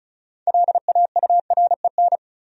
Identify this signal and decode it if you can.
Morse code, heard as a keyed tone, about 36 wpm, 700 Hz. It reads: LAUREN
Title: USA Common Names - 35wpm